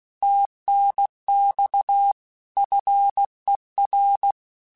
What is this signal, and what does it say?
Morse code, heard as a keyed tone, about 16 wpm, 800 Hz. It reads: TNX FER
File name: QSOs / QSOs-mp3s / WA7QOX 2@15